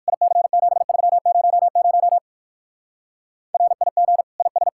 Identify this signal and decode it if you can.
Morse code, heard as a keyed tone, about 56 wpm, 700 Hz. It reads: IC7300 RIG IS
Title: Speed-Racing Common QSO Phrases - 45wpm